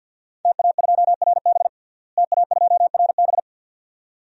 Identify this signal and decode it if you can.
Morse code, heard as a keyed tone, about 47 wpm, 700 Hz. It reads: TA2RB TA2RB
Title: DXCC-100 Call Signs - 5 Characters - Repeated 2x - 50wpm